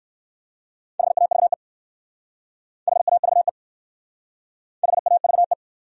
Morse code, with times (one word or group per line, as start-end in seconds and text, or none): 1.0-1.5 HAVE
2.9-3.5 HAVE
4.8-5.5 HAVE